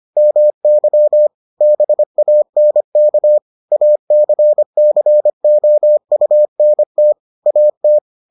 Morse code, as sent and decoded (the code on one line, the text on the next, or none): -- -.-- / -... .- -. -.- / .- -.-. -.-. --- ..- -. - / .- -
MY BANK ACCOUNT AT